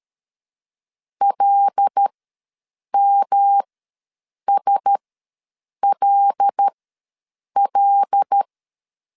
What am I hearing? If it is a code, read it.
Morse code, heard as a keyed tone, about 13 wpm, spaced as for 7 wpm, 800 Hz. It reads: LMSLL
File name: LEZIONI / 5 LBVC / 5 L ws 12wpm 2958